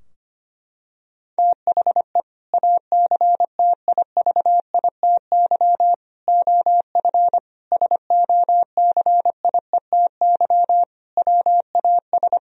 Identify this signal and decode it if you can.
Morse code, heard as a keyed tone, about 25 wpm, 700 Hz. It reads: THE ACTIVITY OF SOCIETY WAS
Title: Sentences from Top 1000 Words - Review Part 4 - 25wpm